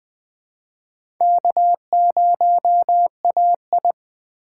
Morse code, heard as a keyed tone, about 20 wpm, 700 Hz. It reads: K0AI
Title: USA Call Signs - 3x Character Spacing - 20wpm